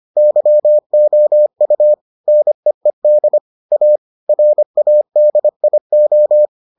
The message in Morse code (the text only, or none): YOU NEED A RADIO